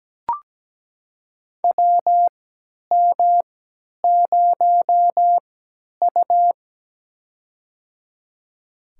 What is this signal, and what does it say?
Morse code, heard as a keyed tone, about 17 wpm, spaced as for 10 wpm, 700 Hz. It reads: WM0U
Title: USA Call Signs - 4 Characters - 3x Character Spacing - 17wpm